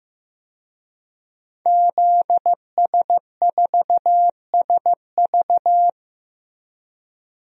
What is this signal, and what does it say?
Morse code, heard as a keyed tone, about 15 wpm, 700 Hz. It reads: ZS4SV
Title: DXCC-100 Call Signs - 5 Characters - 15wpm